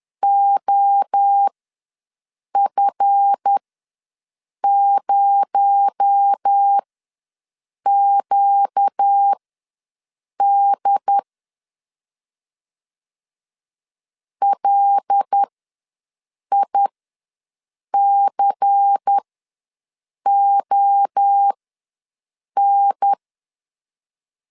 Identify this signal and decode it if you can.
Morse code, heard as a keyed tone, about 10 wpm, spaced as for 6 wpm, 800 Hz. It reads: OF0QD LICON
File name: LEZIONI / 7 GF9Z / 7 GF ws 10wpm 3918